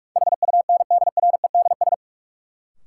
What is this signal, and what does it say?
Morse code, heard as a keyed tone, about 46 wpm, 700 Hz. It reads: HUNDREDS
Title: Sets of 4 Words - New Word Drill - 45wpm